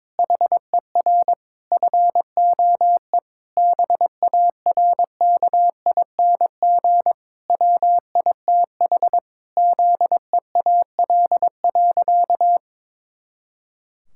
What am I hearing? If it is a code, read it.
Morse code, heard as a keyed tone, about 22 wpm, 700 Hz. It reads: HER FOE BARKING WITH ZEAL.
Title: Pangram Challenge v2 - 22wpm